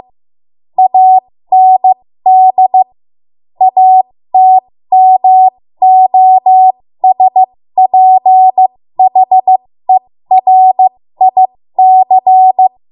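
Morse code, heard as a keyed tone, about 15 wpm, spaced as for 13 wpm, 750 Hz. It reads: AND ATMOSPHERIC
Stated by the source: 13 wpm Morse code practice archive transmission.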